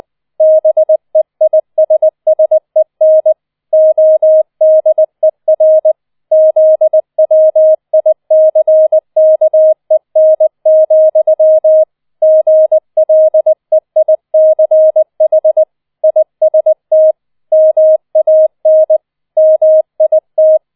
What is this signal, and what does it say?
Morse code, heard as a keyed tone, about 20 wpm, 630 Hz. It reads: BEISSEN ODER ZWICKEN, GLEICH IST MAN MIT
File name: 20wpm